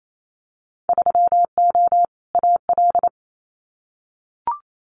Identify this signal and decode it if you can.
Morse code, heard as a keyed tone, about 28 wpm, 700 Hz. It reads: 3O AL
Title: Winter Field Day Exchanges 28wpm